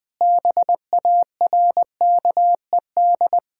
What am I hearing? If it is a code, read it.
Morse code, heard as a keyed tone, about 20 wpm, 700 Hz. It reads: BARKED
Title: Sentences with 2-Word Phrases - 20wpm